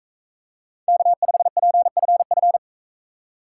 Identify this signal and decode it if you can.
Morse code, heard as a keyed tone, about 38 wpm, 700 Hz. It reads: K5PFF